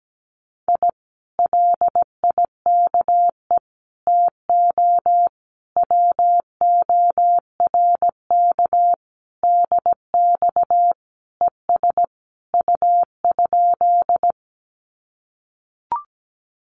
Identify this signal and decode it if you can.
Morse code, heard as a keyed tone, about 17 wpm, 700 Hz. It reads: I LIKE TO WORK DX ES U?
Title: Common QSO Phrases Encore 17wpm